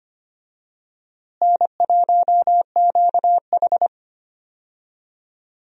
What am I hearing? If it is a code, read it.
Morse code, heard as a keyed tone, about 25 wpm, 700 Hz. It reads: N1QH
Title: USA Call Signs - 4 Characters - Repeated 3x - 25wpm